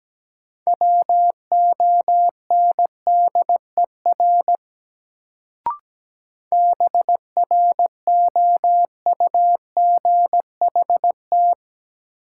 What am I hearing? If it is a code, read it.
Morse code, heard as a keyed tone, about 17 wpm, 700 Hz. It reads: WONDER BROUGHT
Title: Sentences from Top 500 Words - New Word Drill - 17wpm